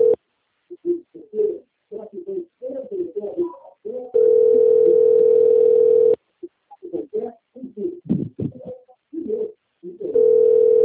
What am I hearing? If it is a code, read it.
Ringback tone (North America), 440 + 480 Hz.